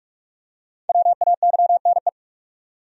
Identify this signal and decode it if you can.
Morse code, heard as a keyed tone, about 46 wpm, 700 Hz. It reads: WAYNE
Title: USA Common Names - 45wpm